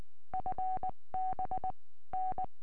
Morse code, heard as a keyed tone, about 19 wpm, 750 Hz. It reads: FB N
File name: tropo / f6dkw 70cm